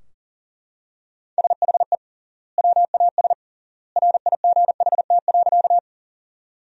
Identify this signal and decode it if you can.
Morse code, heard as a keyed tone, about 40 wpm, 700 Hz. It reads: SHE WAS RIGHT.
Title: 3x Word-Spacing - Sentences from Top 200 Words - 40wpm